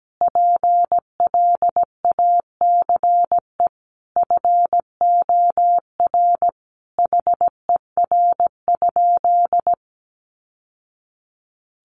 Morse code, heard as a keyed tone, about 17 wpm, 700 Hz. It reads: PLACE FOR HER?
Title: Sentences from Top 800 Words 17wpm